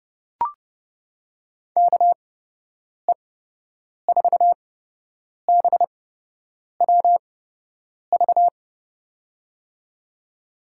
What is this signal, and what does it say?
Morse code, heard as a keyed tone, about 30 wpm, spaced as for 8 wpm, 700 Hz. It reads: KE4BWV
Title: USA Call Signs - 6 Characters - 8x Character Spacing - 30wpm